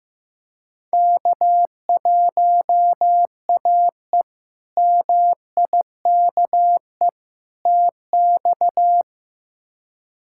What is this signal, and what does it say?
Morse code, heard as a keyed tone, about 15 wpm, 700 Hz. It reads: K1AE MIKE TX
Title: K1USN SST QSO Practice - 15wpm